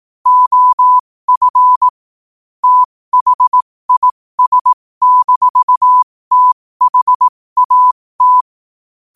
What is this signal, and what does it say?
Morse code, heard as a keyed tone, about 18 wpm, spaced as for 15 wpm, 1000 Hz. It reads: OF THIS-THAT
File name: qotdfiles / 15wpm